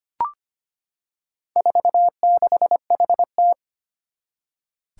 Morse code, heard as a keyed tone, about 25 wpm, 700 Hz. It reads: 46HT
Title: Four Challenging Letter-Number Combinations 25wpm